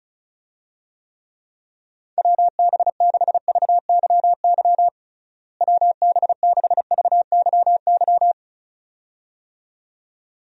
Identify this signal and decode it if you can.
Morse code, heard as a keyed tone, about 35 wpm, 700 Hz. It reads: WB6VYY WB6VYY
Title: USA Call Signs - 6 Characters - Repeated 2x - 35wpm